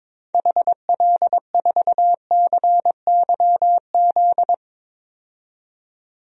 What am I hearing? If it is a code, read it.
Morse code, heard as a keyed tone, about 22 wpm, 700 Hz. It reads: HL4CYZ
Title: DXCC-100 Call Signs - 6 Characters - 22wpm